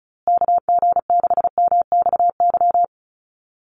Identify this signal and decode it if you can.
Morse code, heard as a keyed tone, about 34 wpm, 700 Hz. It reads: KG6MXY
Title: USA Call Signs Repeated 3x 35wpm